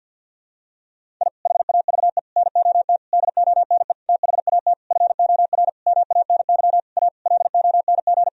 Morse code, heard as a keyed tone, about 49 wpm, 700 Hz. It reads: I HAVE NOT DONE THAT FOR MANY A LONG
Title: Sentences from Top 1000 Words - Review Part 1 - 50wpm